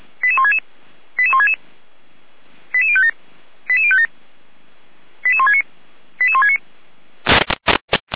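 Five-tone selective call (ZVEI), heard as quick runs of five tones, tones of about 70 ms. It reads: 88166 88166 88057 88057 88169 88169
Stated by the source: ZVEI five-tone selective call received by SDR.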